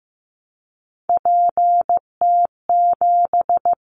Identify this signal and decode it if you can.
Morse code, heard as a keyed tone, about 15 wpm, 700 Hz. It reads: PT7
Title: Callsign Prefixes for Top 50 DXCC - 15wpm